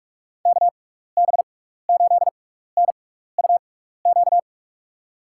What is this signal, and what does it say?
Morse code, heard as a keyed tone, about 44 wpm, spaced as for 15 wpm, 700 Hz. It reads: KB8NUQ
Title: USA Call Signs - 6 Characters - 6x Character Spacing - 45wpm